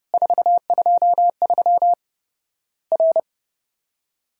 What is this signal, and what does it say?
Morse code, heard as a keyed tone, about 32 wpm, 700 Hz. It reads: 423 R
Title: Worked All Europe DX Contest - QTC Exchange - 30wpm